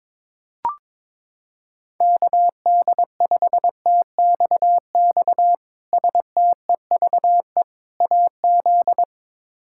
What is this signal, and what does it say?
Morse code, heard as a keyed tone, about 21 wpm, 700 Hz. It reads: KD5TXX STEVE AZ
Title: K1USN SST QSO Practice - 22wpm